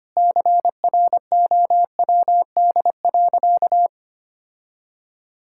Morse code, heard as a keyed tone, about 25 wpm, 700 Hz. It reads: CROWD.
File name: practice / 25wpm / Sentences with 4-Word Phrases - 25wpm